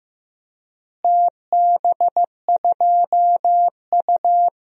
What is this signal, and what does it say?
Morse code, heard as a keyed tone, about 15 wpm, 700 Hz. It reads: TB2U
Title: DXCC-100 Call Signs - 4 Characters - Repeated 2x - 15wpm